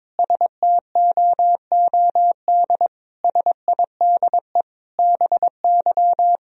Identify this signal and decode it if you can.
Morse code, heard as a keyed tone, about 22 wpm, 700 Hz. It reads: STOOD SIDE BY
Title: Sentences from Top 1000 Words - Review Part 2 - 22wpm